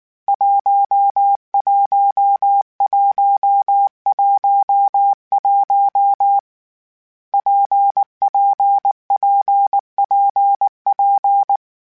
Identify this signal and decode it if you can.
Morse code, heard as a keyed tone, about 19 wpm, 800 Hz. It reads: 11111 PPPPP
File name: Difficult Combos / Difficult Combos-mp3s / WPJ WPJ1 @18